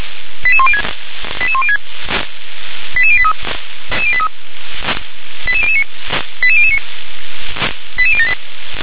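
Five-tone selective call (ZVEI), heard as quick runs of five tones, tones of about 70 ms. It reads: 88106 88106 88083 88009 88009 88078
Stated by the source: ZVEI five-tone call received by SDR.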